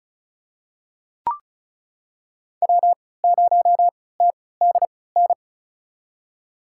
Morse code, heard as a keyed tone, about 36 wpm, spaced as for 20 wpm, 700 Hz. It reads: W0TDN
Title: USA Call Signs - 5 Characters - 3x Character Spacing - 35wpm